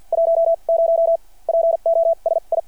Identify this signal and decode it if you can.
Morse code, heard as a keyed tone, about 52 wpm, spaced as for 39 wpm, 660 Hz. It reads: 10 POSI
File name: archived / CRSnet